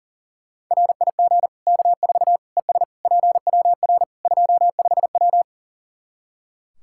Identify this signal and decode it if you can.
Morse code, heard as a keyed tone, about 40 wpm, 700 Hz. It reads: RIG K4 ES PWR 25W
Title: Practice QSO - Exchange 2 - Info - 40wpm